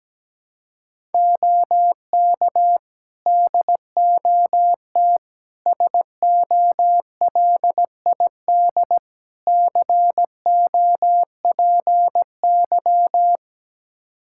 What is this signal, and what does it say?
Morse code, heard as a keyed tone, about 17 wpm, 700 Hz. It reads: OK DOT SOLID COPY
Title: Practice QSO - Exchange 3 - More Info - 17wpm